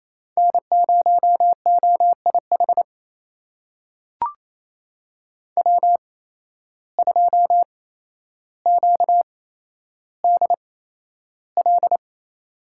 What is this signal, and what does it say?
Morse code, heard as a keyed tone, about 27 wpm, spaced as for 12 wpm, 700 Hz. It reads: N0OIH W2QDL
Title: USA Call Signs - 5 Characters - 8x Character Spacing - 28wpm